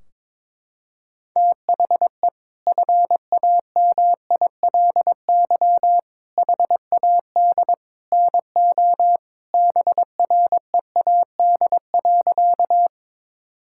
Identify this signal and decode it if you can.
Morse code, heard as a keyed tone, about 22 wpm, 700 Hz. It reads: THE FAMILY HAD NO BREAD.
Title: Sentences from Top 1600 Words 22wpm